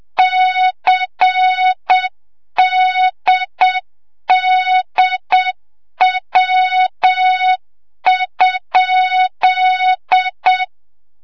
Morse code, heard as a keyed tone, about 7 wpm, 3700 Hz, with noise. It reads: CDDW?
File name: Morze-HA2MN / hang / 25 GYAKORLAT